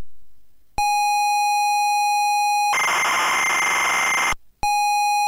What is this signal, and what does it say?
ZX Spectrum tape loading.